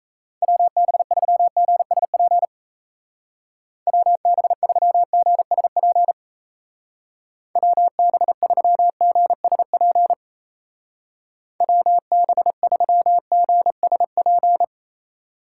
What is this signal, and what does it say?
Morse code, heard as a keyed tone, about 35 wpm, 700 Hz. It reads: WB3GSP WB3GSP WB3GSP WB3GSP
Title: Speed Racing - USA Call Signs 28wpm